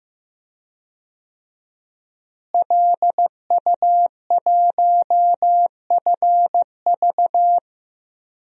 Morse code, heard as a keyed tone, about 16 wpm, 700 Hz. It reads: LU1FV
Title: DXCC-100 Call Signs - 5 Characters - Repeated 3x - 15wpm